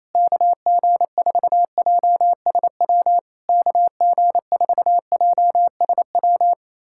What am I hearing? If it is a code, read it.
Morse code, heard as a keyed tone, about 28 wpm, 700 Hz. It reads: KG4JSW KG4JSW
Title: SOTA QSO Practice 28wpm